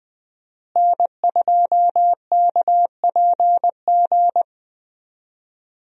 Morse code, heard as a keyed tone, about 20 wpm, 700 Hz. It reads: N2KPG
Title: USA Call Signs - 5 Characters - Repeated 2x - 20wpm